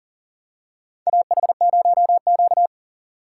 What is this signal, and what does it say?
Morse code, heard as a keyed tone, about 40 wpm, 700 Hz. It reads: AH0Q
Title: USA Call Signs - 4x Character Spacing - 40wpm